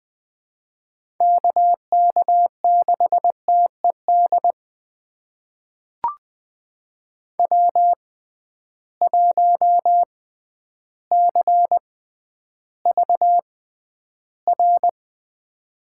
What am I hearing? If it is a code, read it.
Morse code, heard as a keyed tone, about 20 wpm, spaced as for 11 wpm, 700 Hz. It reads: KK6TED W1CVR
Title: USA Call Signs - 6x Character Spacing - 20wpm